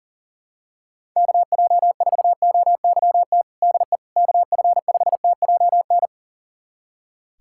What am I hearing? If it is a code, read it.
Morse code, heard as a keyed tone, about 40 wpm, 700 Hz. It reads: KJ4OYT DE KF5TJN
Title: USA Callsign DE Callsign - 6 Characters - 40wpm